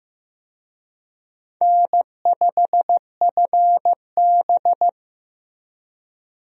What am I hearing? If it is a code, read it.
Morse code, heard as a keyed tone, about 15 wpm, 700 Hz. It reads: N5FB
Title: USA Call Signs - 4 Characters - Repeated 2x - 15wpm